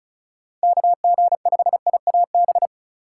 Morse code, heard as a keyed tone, about 36 wpm, 690 Hz. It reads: KG5IAB
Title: USA Call Signs - 6 Characters - 3x Character Spacing - 35wpm